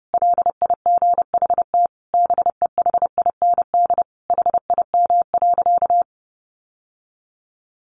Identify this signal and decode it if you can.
Morse code, heard as a keyed tone, about 30 wpm, 700 Hz. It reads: LIGHT BEHIND HIM.
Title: Sentences from Top 1000 Words 30wpm